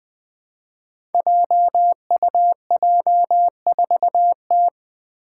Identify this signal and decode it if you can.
Morse code, heard as a keyed tone, about 20 wpm, 700 Hz. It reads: JUJ4T
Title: Five Challenging Letter-Number Combinations 20wpm